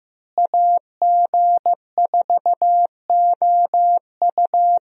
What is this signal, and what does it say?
Morse code, heard as a keyed tone, about 15 wpm, 700 Hz. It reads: AG4OU
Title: USA Call Signs - 5 Characters - Repeated 3x - 15wpm